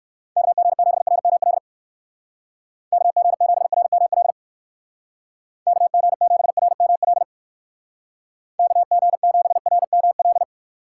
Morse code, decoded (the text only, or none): KG7RML KG7RML KG7RML KG7RML